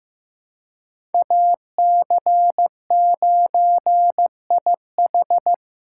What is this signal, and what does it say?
Morse code, heard as a keyed tone, about 15 wpm, 700 Hz. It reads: AC9IH
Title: USA Call Signs - 5 Characters - 3x Character Spacing - 15wpm